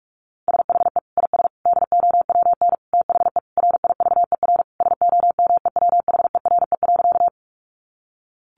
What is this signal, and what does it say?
Morse code, heard as a keyed tone, about 44 wpm, 700 Hz. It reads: SHE IS DOWN THE RIVER SOMEWHERE.
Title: Sentences from Top 1400 Words 45wpm